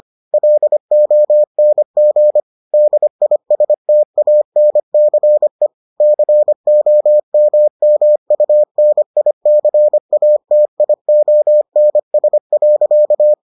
Morse code, heard as a keyed tone, about 25 wpm, 600 Hz. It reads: LONG DISTANCE COMMUNICATIONS.